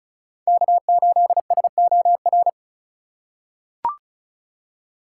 Morse code, heard as a keyed tone, about 34 wpm, 700 Hz. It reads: K8SOR
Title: USA Call Signs - 5 Characters - 4x Character Spacing - 35wpm